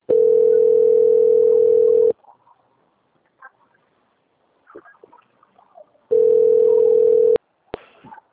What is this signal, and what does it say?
Ringback tone (North America), 440 + 480 Hz.